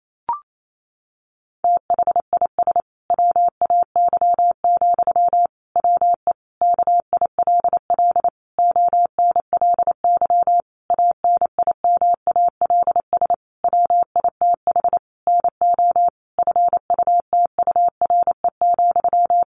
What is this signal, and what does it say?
Morse code, heard as a keyed tone, about 28 wpm, 700 Hz. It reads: THIS WAY, WE KILL ONLY ANIMALS WITH NO FUTURE,
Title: A Sound Of Thunder 28wpm